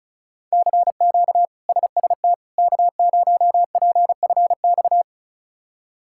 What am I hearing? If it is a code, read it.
Morse code, heard as a keyed tone, about 36 wpm, 700 Hz. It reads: CQ SST K0PFX
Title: K1USN SST QSO Practice - 35wpm